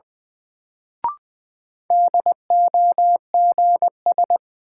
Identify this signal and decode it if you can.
Morse code, heard as a keyed tone, about 20 wpm, 700 Hz. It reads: DOGS
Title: Sentences from Top 1600 Words - New Word Drill - 20wpm